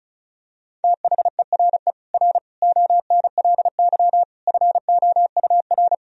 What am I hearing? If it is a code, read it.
Morse code, heard as a keyed tone, about 36 wpm, 700 Hz. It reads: THERE R ONLY FOUR
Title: Sentences from Top 300 Words and Abbreviations 35wpm